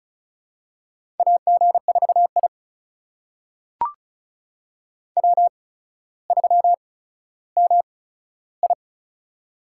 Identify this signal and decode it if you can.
Morse code, heard as a keyed tone, about 36 wpm, spaced as for 15 wpm, 700 Hz. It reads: AG4I W3MI
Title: USA Call Signs - 4 Characters - 8x Character Spacing - 35wpm